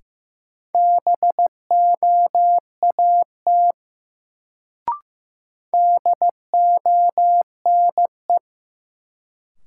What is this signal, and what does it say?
Morse code, heard as a keyed tone, about 15 wpm, 700 Hz. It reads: BOAT DONE
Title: Sentences from Top 500 Words - New Word Drill - 15wpm